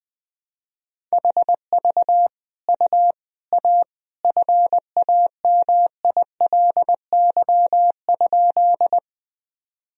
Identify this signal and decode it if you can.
Morse code, heard as a keyed tone, about 20 wpm, 700 Hz. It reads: HV U A FAMILY?
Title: Sentences from Top 400 Words and Abbreviations 20wpm